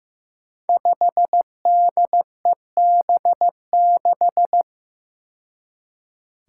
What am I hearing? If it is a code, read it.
Morse code, heard as a keyed tone, about 15 wpm, 700 Hz. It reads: 5DEB6
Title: Five Challenging Letter-Number Combinations 15wpm